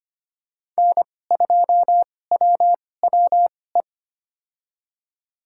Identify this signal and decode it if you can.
Morse code, heard as a keyed tone, about 25 wpm, spaced as for 18 wpm, 700 Hz. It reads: N2WWE
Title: USA Call Signs - 5 Characters - 2x Character Spacing - 25wpm